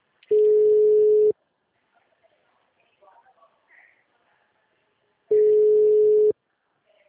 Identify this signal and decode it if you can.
Ringback tone (Europe, 425 Hz), with speech.